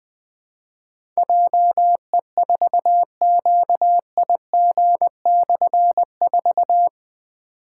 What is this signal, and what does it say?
Morse code, heard as a keyed tone, about 21 wpm, 700 Hz. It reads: JE4QIG/4
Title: Difficult Call Signs 20wpm